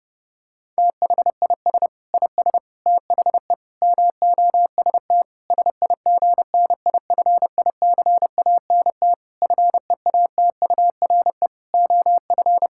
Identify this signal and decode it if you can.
Morse code, heard as a keyed tone, about 30 wpm, 700 Hz. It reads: THIS IS THE MOST SIGNIFICANT FEATURE OF
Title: Sentences from Top 1000 Words - Review Part 6 - 30wpm